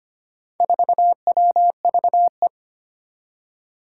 Morse code, heard as a keyed tone, about 25 wpm, 700 Hz. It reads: 4WVE
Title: Four Challenging Letter-Number Combinations 25wpm